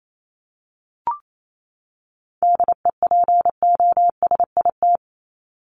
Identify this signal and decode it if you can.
Morse code, heard as a keyed tone, about 28 wpm, 700 Hz. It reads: DEPOSIT
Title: Top 1251-1500 Words 28wpm